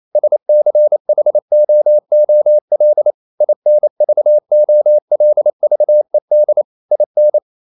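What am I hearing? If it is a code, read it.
Morse code, heard as a keyed tone, about 28 wpm, 600 Hz. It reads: SCHOOL INVOLVED IN
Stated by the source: Morse code practice at 28 wpm, text of Ken Follett's novel Never in English, from CW med Gnister.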